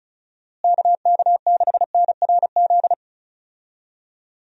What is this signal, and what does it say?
Morse code, heard as a keyed tone, about 36 wpm, 700 Hz. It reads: KK6NRZ
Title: USA Call Signs - 6 Characters - Repeated 2x - 35wpm